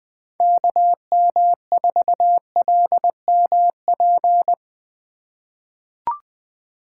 Morse code, heard as a keyed tone, about 20 wpm, 700 Hz. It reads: KM4LMP
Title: USA Call Signs - 6 Characters - 3x Character Spacing - 20wpm